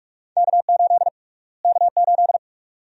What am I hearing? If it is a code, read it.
Morse code, heard as a keyed tone, about 47 wpm, 700 Hz. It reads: K8 K8
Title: USA Call Sign Prefixes - Repeated 2x - 45wpm